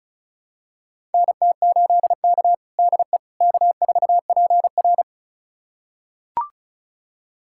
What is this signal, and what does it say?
Morse code, heard as a keyed tone, about 36 wpm, 700 Hz. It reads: NT8K DE K4PR
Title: USA Callsign DE Callsign - 4 Characters - 35wpm